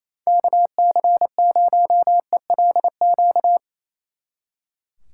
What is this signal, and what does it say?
Morse code, heard as a keyed tone, about 34 wpm, 700 Hz. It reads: KC0ELQ
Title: Speed Racing - USA 2x3 Call Signs 28wpm